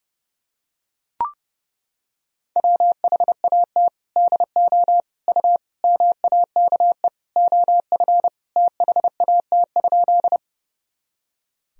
Morse code, heard as a keyed tone, about 32 wpm, 700 Hz. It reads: WHAT DO U MAKE OF THAT?
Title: Sentences from Top 100 Words and Abbreviations 30wpm